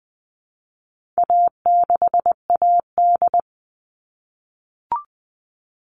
Morse code, heard as a keyed tone, about 20 wpm, 700 Hz. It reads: A6AD
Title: All Cut Numbers - 4-Digit 20wpm